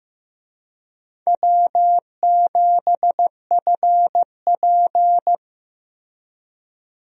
Morse code, heard as a keyed tone, about 15 wpm, 700 Hz. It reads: W7FP
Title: USA Call Signs - 4 Characters - Repeated 3x - 15wpm